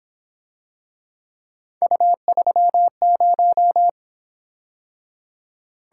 Morse code, heard as a keyed tone, about 26 wpm, 700 Hz. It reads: U30
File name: Speed-Racing - Three Letter-Number Combinations - 22wpm